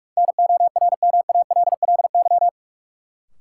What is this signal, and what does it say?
Morse code, heard as a keyed tone, about 45 wpm, 700 Hz. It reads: NORMALLY
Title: Sentences from Top 1000 Words - Review Part 6 - New Word Drill - 45wpm